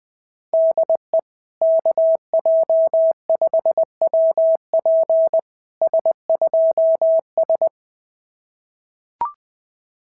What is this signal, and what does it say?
Morse code, heard as a keyed tone, about 20 wpm, 650 Hz. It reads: DE KJ5WP S2S
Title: SOTA QSO Practice 20wpm